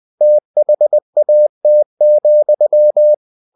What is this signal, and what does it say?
Morse code, heard as a keyed tone, about 20 wpm, 600 Hz. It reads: THAT,
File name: cw-books / winnie-the-pooh / winnie-20wpm-page-0017